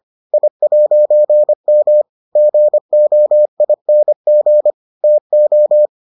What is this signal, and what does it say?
Morse code, heard as a keyed tone, about 25 wpm, 600 Hz. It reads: I'M GOING TO